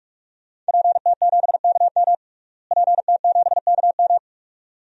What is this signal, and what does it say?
Morse code, heard as a keyed tone, about 46 wpm, 700 Hz. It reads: PT7KM PT7KM
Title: DXCC-100 Call Signs - 5 Characters - Repeated 2x - 45wpm